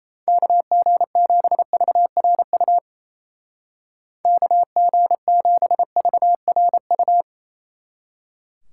Morse code, heard as a keyed tone, about 34 wpm, 700 Hz. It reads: KG7VRU KG7VRU
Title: Speed Racing - USA 2x3 Call Signs 28wpm